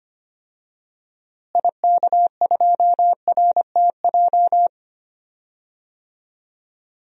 Morse code, heard as a keyed tone, about 25 wpm, 700 Hz. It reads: IK2RTJ
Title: DXCC-100 Call Signs - 6 Characters - Repeated 2x - 25wpm